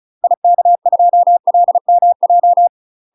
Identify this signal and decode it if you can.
Morse code, heard as a keyed tone, about 36 wpm, 700 Hz. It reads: IK2LMJ
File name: UK MASTER QSOs TRAINING / 35WPM / UK MASTER QSOs 0041